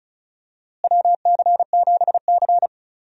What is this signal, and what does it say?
Morse code, heard as a keyed tone, about 34 wpm, 700 Hz. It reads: WC7C
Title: USA Call Signs - 4 Characters - Repeated 3x - 35wpm